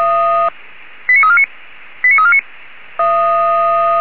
Five-tone selective call (ZVEI), heard as quick runs of five tones, tones of about 70 ms. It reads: 89349 89349